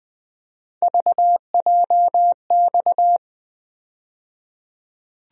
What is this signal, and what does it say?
Morse code, heard as a keyed tone, about 21 wpm, 700 Hz. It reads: VJX
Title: USA Call Sign Postfixes - 3 Letters 20wpm